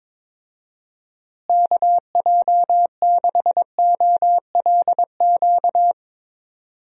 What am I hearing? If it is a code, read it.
Morse code, heard as a keyed tone, about 22 wpm, 700 Hz. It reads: KJ6OLQ
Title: USA Call Signs - 6 Characters - Repeated 2x - 22wpm